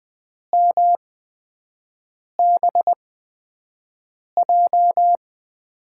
Morse code, heard as a keyed tone, about 20 wpm, spaced as for 5 wpm, 700 Hz. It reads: MBJ
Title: USA Call Signs - 5 Characters - 8x Character Spacing - 20wpm